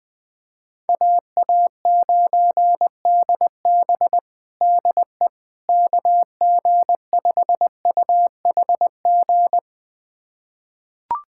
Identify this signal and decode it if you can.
Morse code, heard as a keyed tone, about 21 wpm, 700 Hz. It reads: AA9DB DE KG5UHG
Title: USA Callsign DE Callsign 20wpm